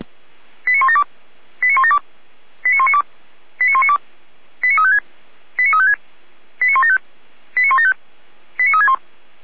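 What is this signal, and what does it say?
Five-tone selective call (ZVEI), heard as quick runs of five tones, tones of about 70 ms. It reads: 89172 89172 89192 89192 89356 89356 89175 89175 89361
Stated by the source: ZVEI five-tone call 89560 received with an SDR.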